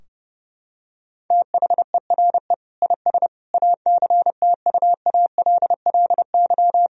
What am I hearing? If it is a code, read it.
Morse code, heard as a keyed tone, about 30 wpm, 700 Hz. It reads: THERE IS ACTUALLY
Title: Sentences from Top 1000 Words - Review Part 4 - 30wpm